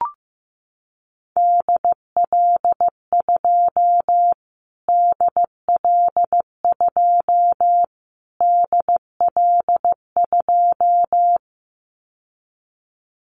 Morse code, heard as a keyed tone, about 15 wpm, 700 Hz. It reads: DL2 DL2 DL2
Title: Callsign Prefixes for Top 25 DXCC - Repeated 3x - 15wpm